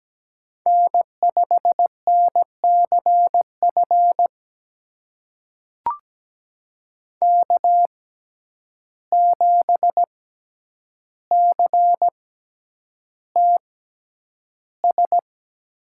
Morse code, heard as a keyed tone, about 17 wpm, spaced as for 9 wpm, 700 Hz. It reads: N5NCF K7CTS
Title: USA Call Signs - 5 Characters - 6x Character Spacing - 17wpm